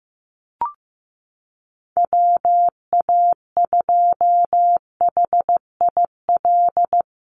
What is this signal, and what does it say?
Morse code, heard as a keyed tone, about 15 wpm, 700 Hz. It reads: WA2HIL
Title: USA Call Signs Repeated 3x 15wpm